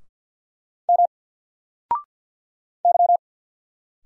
Morse code, heard as a keyed tone, about 50 wpm, 700 Hz. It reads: M Y
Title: Single Letter-Number 50wpm